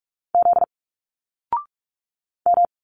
Morse code, heard as a keyed tone, about 46 wpm, 700 Hz. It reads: Z M
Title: Single Letters 45wpm